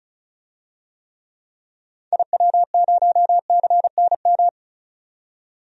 Morse code, heard as a keyed tone, about 34 wpm, 700 Hz. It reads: IW0CNM
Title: DXCC-100 Call Signs - 6 Characters - 35wpm